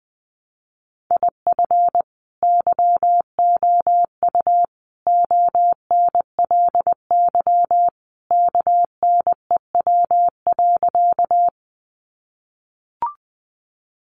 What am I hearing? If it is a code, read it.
Morse code, heard as a keyed tone, about 20 wpm, 700 Hz. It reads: IF YOU ONLY KNEW.
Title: Sentences from Top 400 Words 20wpm